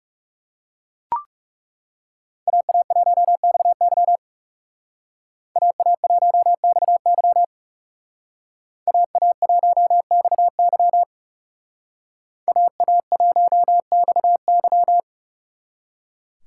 Morse code, heard as a keyed tone, about 39 wpm, 700 Hz. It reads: AA1XY AA1XY AA1XY AA1XY